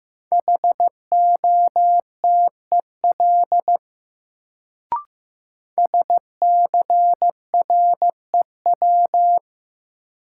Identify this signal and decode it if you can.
Morse code, heard as a keyed tone, about 15 wpm, 700 Hz. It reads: HOTEL SCREW
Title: Sentences from Top 1600 Words - New Word Drill Alt - 15wpm